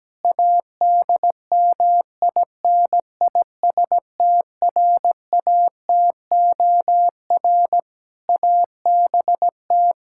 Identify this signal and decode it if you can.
Morse code, heard as a keyed tone, about 17 wpm, 700 Hz. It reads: ADMINISTRATOR ABT
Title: Practice QSO - Exchange 3 - More Info - 17wpm